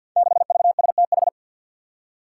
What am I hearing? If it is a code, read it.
Morse code, heard as a keyed tone, about 53 wpm, 700 Hz. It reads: 6VSTH
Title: Five Challenging Letter-Number Combinations 50wpm